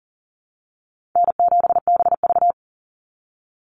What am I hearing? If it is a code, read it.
Morse code, heard as a keyed tone, about 38 wpm, 700 Hz. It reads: N7BV
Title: Easy Call Signs Repeated 2x - 40wpm